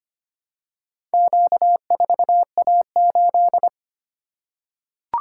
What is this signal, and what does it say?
Morse code, heard as a keyed tone, about 25 wpm, 700 Hz. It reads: Q4A8
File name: practice / 25wpm / Four Letter-Number Combinations - 25wpm